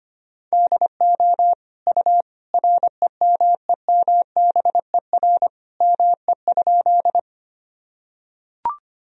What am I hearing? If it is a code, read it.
Morse code, heard as a keyed tone, about 25 wpm, 700 Hz. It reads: DO U REMEMBER ME?
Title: Sentences from Top 400 Words and Abbreviations 25wpm